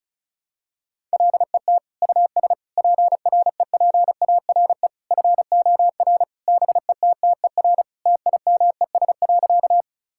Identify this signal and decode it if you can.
Morse code, heard as a keyed tone, about 36 wpm, 700 Hz. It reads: LET US PREPARE FOR BETTER TIMES.
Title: Sentences from Top 1000 Words - Review Part 6 - 35wpm